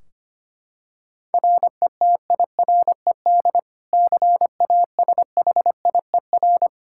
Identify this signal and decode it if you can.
Morse code, heard as a keyed tone, about 25 wpm, 700 Hz. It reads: RETIRED CASHIER
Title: Practice QSO - Exchange 3 - More Info - 25wpm